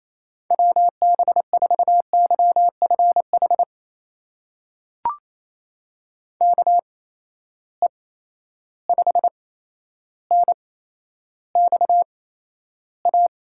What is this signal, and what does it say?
Morse code, heard as a keyed tone, about 28 wpm, spaced as for 12 wpm, 700 Hz. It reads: WB4YFH KE5NXA
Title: USA Call Signs - 6 Characters - 8x Character Spacing - 28wpm